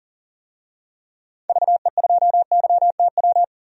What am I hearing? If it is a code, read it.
Morse code, heard as a keyed tone, about 37 wpm, 700 Hz. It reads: VE2YTW
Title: DXCC-100 Call Signs - 6 Characters - 40wpm